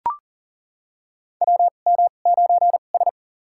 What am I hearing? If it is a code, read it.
Morse code, heard as a keyed tone, about 40 wpm, spaced as for 29 wpm, 700 Hz. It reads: WM9S
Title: USA Call Signs - 2x Character Spacing - 40wpm